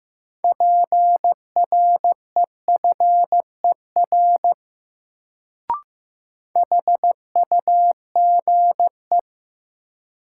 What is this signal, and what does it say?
Morse code, heard as a keyed tone, about 15 wpm, 700 Hz. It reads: PREFER HUGE
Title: Sentences from Top 1100 Words - New Word Drill Alt - 15wpm